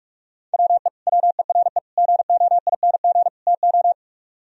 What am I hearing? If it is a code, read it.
Morse code, heard as a keyed tone, about 47 wpm, 700 Hz. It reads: WE WERE GOING TO